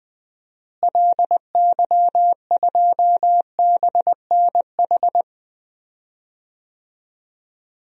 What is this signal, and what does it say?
Morse code, heard as a keyed tone, about 20 wpm, 700 Hz. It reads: LY2BNH
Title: DXCC-100 Call Signs - 6 Characters - Repeated 3x - 20wpm